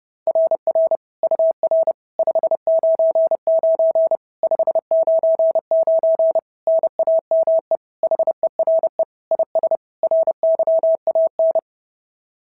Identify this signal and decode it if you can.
Morse code, heard as a keyed tone, about 26 wpm, 650 Hz. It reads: RR UR 599 599 NAME HERE IS RYAN
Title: SOTA QSO Practice 30wpm